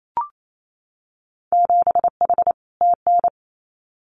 Morse code, heard as a keyed tone, about 28 wpm, 700 Hz. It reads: 7H TN
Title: Winter Field Day Exchanges Encore 28wpm